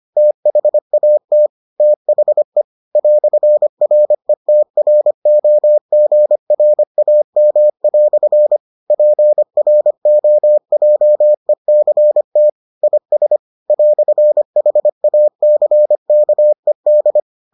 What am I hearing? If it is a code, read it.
Morse code, heard as a keyed tone, about 25 wpm, 600 Hz. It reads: THAT THE "RETROGRAM" PROJECT IS "HACKED